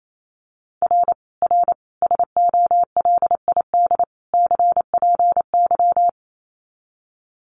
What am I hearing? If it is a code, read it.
Morse code, heard as a keyed tone, about 28 wpm, 700 Hz. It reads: R R SOLID CPY
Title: Common QSO Phrases 28wpm